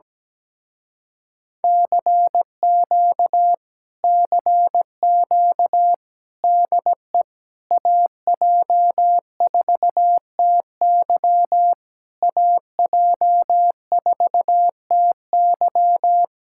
Morse code, heard as a keyed tone, about 17 wpm, 700 Hz. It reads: CQ CQ DE AJ4TY AJ4TY
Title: Speed-Racing Common QSO Phrases - 17wpm